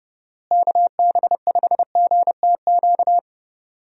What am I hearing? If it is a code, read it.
Morse code, heard as a keyed tone, about 30 wpm, 700 Hz. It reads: KB5GTQ
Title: USA Call Signs - 6 Characters - Repeated 2x - 30wpm